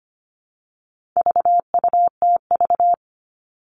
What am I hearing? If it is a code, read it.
Morse code, heard as a keyed tone, about 25 wpm, 700 Hz. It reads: VUTV
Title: All Cut Numbers - 4-Digit 25wpm